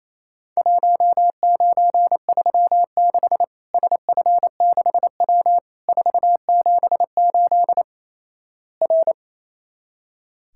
Morse code, heard as a keyed tone, about 28 wpm, 700 Hz. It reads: 1936 SF6W 478 R
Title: Worked All Europe DX Contest - QTC Exchange - 28wpm